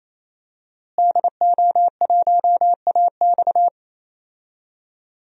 Morse code, heard as a keyed tone, about 28 wpm, 700 Hz. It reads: DO1AX
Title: DXCC-100 Call Signs - 5 Characters - Repeated 3x - 28wpm